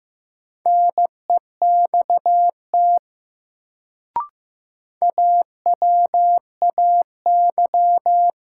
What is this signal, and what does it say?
Morse code, heard as a keyed tone, about 15 wpm, 700 Hz. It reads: NEXT AWAY
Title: Sets of 2 Words - New Word Drill - 15wpm